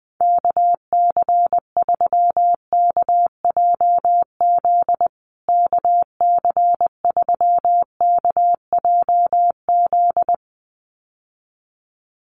Morse code, heard as a keyed tone, about 20 wpm, 700 Hz. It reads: KC3KJZ KC3KJZ
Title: USA Call Signs Repeated 2x 20wpm